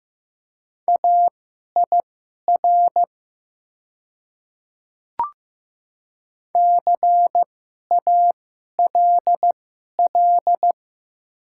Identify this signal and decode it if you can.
Morse code, heard as a keyed tone, about 15 wpm, spaced as for 11 wpm, 700 Hz. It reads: AIR CALL
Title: 2x Character Spacing - Top 300 Words - 15wpm